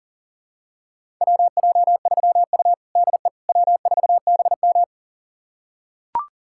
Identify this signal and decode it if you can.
Morse code, heard as a keyed tone, about 40 wpm, 700 Hz. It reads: WJ3U DE W4BM
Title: USA Callsign DE Callsign - 4 Characters - 40wpm